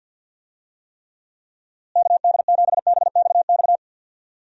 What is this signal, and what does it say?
Morse code, heard as a keyed tone, about 52 wpm, 700 Hz. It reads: KD7BXX
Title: USA Call Signs - 6 Characters - Repeated 3x - 50wpm